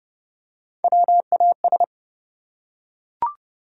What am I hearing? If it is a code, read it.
Morse code, heard as a keyed tone, about 30 wpm, 700 Hz. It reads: WAS
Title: Top 100 Words 30wpm